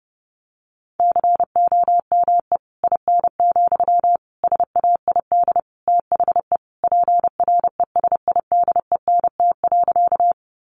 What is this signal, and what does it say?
Morse code, heard as a keyed tone, about 30 wpm, 700 Hz. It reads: COME IN, SAID THE PRESIDENT.
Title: Sentences from Top 1000 Words 30wpm